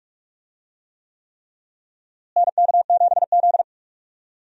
Morse code, heard as a keyed tone, about 47 wpm, 700 Hz. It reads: NK7Z
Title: USA Call Signs - 4 Characters - Repeated 3x - 45wpm